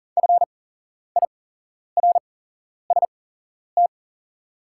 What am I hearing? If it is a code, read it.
Morse code, heard as a keyed tone, about 40 wpm, spaced as for 11 wpm, 700 Hz. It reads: FIRST
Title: 8x Character Spacing - Top 100 Words - 40wpm